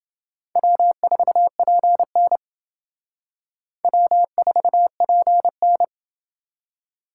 Morse code, heard as a keyed tone, about 28 wpm, 700 Hz. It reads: W4PN W4PN